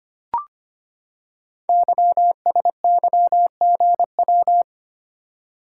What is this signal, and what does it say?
Morse code, heard as a keyed tone, about 25 wpm, 700 Hz. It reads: YSYGW
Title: Code Groups - 5 Letters - 25wpm